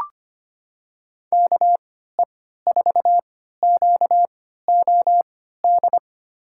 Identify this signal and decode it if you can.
Morse code, heard as a keyed tone, about 25 wpm, spaced as for 14 wpm, 700 Hz. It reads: KE4QOD
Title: USA Call Signs - 3x Character Spacing - 25wpm